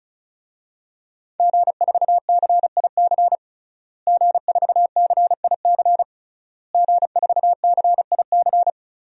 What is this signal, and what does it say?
Morse code, heard as a keyed tone, about 36 wpm, 700 Hz. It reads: G4CIC G4CIC G4CIC
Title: DXCC-100 Call Signs - 5 Characters - Repeated 3x - 35wpm